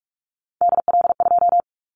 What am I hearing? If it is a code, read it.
Morse code, heard as a keyed tone, about 44 wpm, 700 Hz. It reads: DL2
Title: Callsign Prefixes for Top 50 DXCC - 45wpm